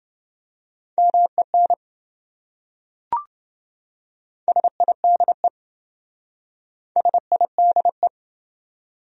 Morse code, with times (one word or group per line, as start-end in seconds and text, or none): 1.0-1.7 MEN
4.5-5.5 SIDE
7.0-8.1 SIDE